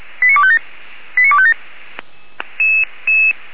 Five-tone selective call (ZVEI), heard as quick runs of five tones, tones of about 70 ms. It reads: 89257 89257
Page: SDR/ZVEI